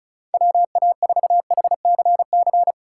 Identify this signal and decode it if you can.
Morse code, heard as a keyed tone, about 36 wpm, 700 Hz. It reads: WA4HCC
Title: K1USN SST QSO Practice - 35wpm